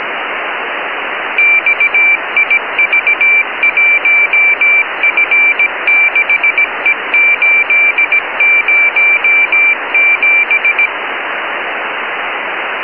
Morse code, heard as a keyed tone, about 17 wpm, 2350 Hz, with radio noise. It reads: XIV1F6E807